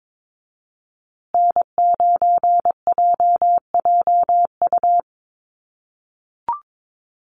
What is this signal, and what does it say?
Morse code, heard as a keyed tone, about 22 wpm, 700 Hz. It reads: N9JJU
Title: USA Call Signs Repeated 3x 22wpm